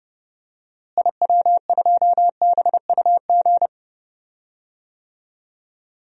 Morse code, heard as a keyed tone, about 30 wpm, 700 Hz. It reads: IW2BUG
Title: DXCC-100 Call Signs - 6 Characters - Repeated 2x - 30wpm